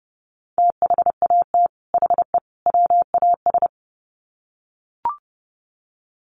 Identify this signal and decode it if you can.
Morse code, heard as a keyed tone, about 30 wpm, 700 Hz. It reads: THAT HE WAS
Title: Sets of 3 Words 30wpm